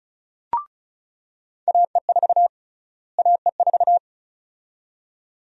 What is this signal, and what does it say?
Morse code, heard as a keyed tone, about 34 wpm, 700 Hz. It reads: AE4 AE4
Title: USA Call Sign Prefixes - Repeated 2x - 35wpm